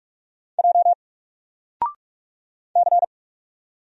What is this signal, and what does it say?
Morse code, heard as a keyed tone, about 44 wpm, 700 Hz. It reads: J C
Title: Single Character (Letter:Number:Punctuation:Prosign) 45wpm